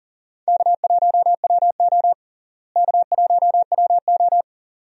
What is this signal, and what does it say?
Morse code, heard as a keyed tone, about 40 wpm, 700 Hz. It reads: K1WO K1WO
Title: USA Call Signs - 4 Characters - Repeated 2x - 40wpm